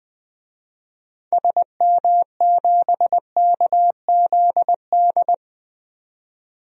Morse code, heard as a keyed tone, about 20 wpm, 700 Hz. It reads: SM7KZD
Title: DXCC-100 Call Signs - 6 Characters - 20wpm